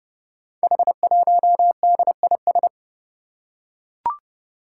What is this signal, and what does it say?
Morse code, heard as a keyed tone, about 28 wpm, 700 Hz. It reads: H1DIS
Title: Five Challenging Letter-Number Combinations 30wpm